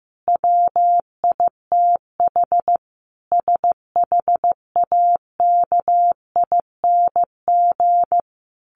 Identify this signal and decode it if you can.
Morse code, heard as a keyed tone, about 15 wpm, 700 Hz. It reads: WITH SHAKING
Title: A Sound Of Thunder 15wpm